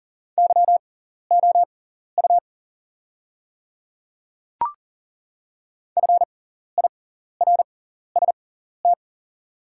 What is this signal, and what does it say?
Morse code, heard as a keyed tone, about 40 wpm, spaced as for 14 wpm, 690 Hz. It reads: YOU FIRST
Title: 6x Character Spacing - Top 100 Words - 40wpm